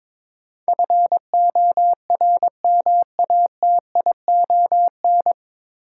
Morse code, heard as a keyed tone, about 22 wpm, 700 Hz. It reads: FORMATION
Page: Morse Code Ninja practice 22wpm